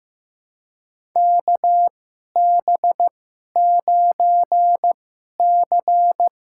Morse code, heard as a keyed tone, about 15 wpm, spaced as for 11 wpm, 700 Hz. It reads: KB9C
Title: USA Call Signs - 6 Characters - 2x Character Spacing - 15wpm